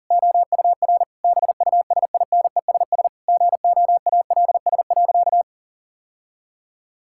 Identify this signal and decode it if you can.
Morse code, heard as a keyed tone, about 40 wpm, 690 Hz. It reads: OUR BUSINESS GOALS.
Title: Sentences from Top 1000 Words - Review Part 6 - 40wpm